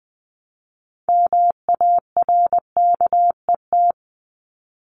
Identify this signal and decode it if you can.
Morse code, heard as a keyed tone, about 20 wpm, 700 Hz. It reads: MARKET
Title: ING Suffix - Top 100 Words with Base Words 20wpm